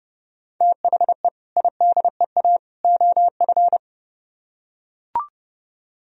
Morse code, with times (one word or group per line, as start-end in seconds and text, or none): 0.6-1.3 THE
1.6-2.6 IDEA
2.8-3.8 OF